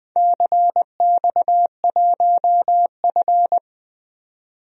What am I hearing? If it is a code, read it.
Morse code, heard as a keyed tone, about 20 wpm, 700 Hz. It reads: CX1F
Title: DXCC-100 Call Signs - 4 Characters - 20wpm